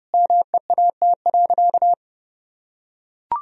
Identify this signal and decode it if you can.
Morse code, heard as a keyed tone, about 32 wpm, 700 Hz. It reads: MEAT.
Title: Sentences from Top 1000 Words - Review Part 3 - 30wpm